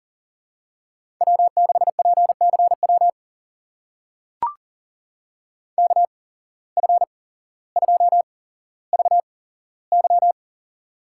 Morse code, heard as a keyed tone, about 41 wpm, spaced as for 17 wpm, 700 Hz. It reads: W6PCW KF2VY
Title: USA Call Signs - 5 Characters - 8x Character Spacing - 40wpm